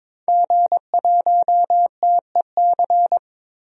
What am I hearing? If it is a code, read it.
Morse code, heard as a keyed tone, about 22 wpm, 700 Hz. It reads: G1TEC
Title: DXCC-100 Call Signs - 5 Characters - Repeated 3x - 22wpm